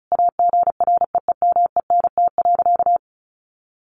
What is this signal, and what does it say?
Morse code, heard as a keyed tone, about 34 wpm, 700 Hz. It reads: AGREEMENT.
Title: Sentences from Top 1500 Words 35wpm